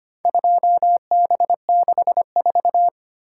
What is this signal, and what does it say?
Morse code, heard as a keyed tone, about 25 wpm, 700 Hz. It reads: 2B64
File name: practice / 25wpm / Four Letter-Number Combinations - 25wpm